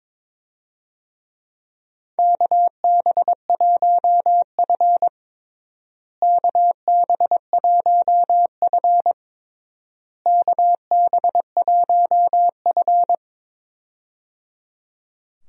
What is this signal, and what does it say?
Morse code, heard as a keyed tone, about 22 wpm, 700 Hz. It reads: KB1F KB1F KB1F
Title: USA Call Signs - 4 Characters - Repeated 3x - 22wpm